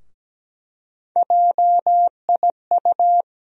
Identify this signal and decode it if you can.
Morse code, heard as a keyed tone, about 17 wpm, 700 Hz. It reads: JIU
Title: USA Call Sign Postfixes - 3 Letters 17wpm